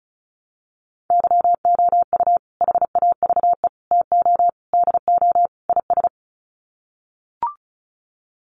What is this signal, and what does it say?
Morse code, heard as a keyed tone, about 34 wpm, 700 Hz. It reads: YOU HAVE TO DO IS
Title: Sets of 5 Words 35wpm